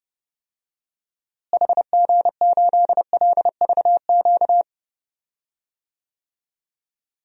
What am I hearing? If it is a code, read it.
Morse code, heard as a keyed tone, about 30 wpm, 700 Hz. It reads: HG8LVQ
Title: DXCC-100 Call Signs - 6 Characters - Repeated 3x - 30wpm